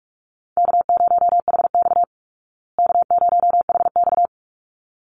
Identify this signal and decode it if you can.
Morse code, heard as a keyed tone, about 44 wpm, 700 Hz. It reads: K0HX K0HX
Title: Easy Call Signs Repeated 2x - 45wpm